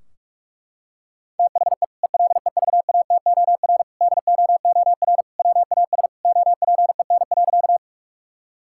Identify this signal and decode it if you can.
Morse code, heard as a keyed tone, about 41 wpm, 700 Hz. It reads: THE ELEVATOR DOOR WAS OPEN.
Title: Sentences from Top 1600 Words 45wpm